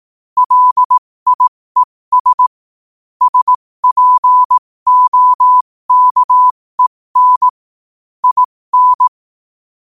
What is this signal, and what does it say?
Morse code, heard as a keyed tone, about 18 wpm, spaced as for 15 wpm, 1000 Hz. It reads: LIES SPOKEN IN